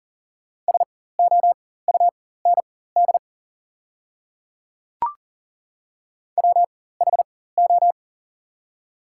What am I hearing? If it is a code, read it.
Morse code, heard as a keyed tone, about 41 wpm, spaced as for 19 wpm, 690 Hz. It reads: SOUND WHO
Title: 4x Character Spacing - Top 100 Words - 40wpm